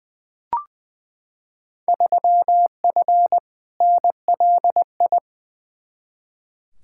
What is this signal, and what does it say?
Morse code, heard as a keyed tone, about 20 wpm, 700 Hz. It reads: 3F NLI
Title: Field Day Exchanges - 20wpm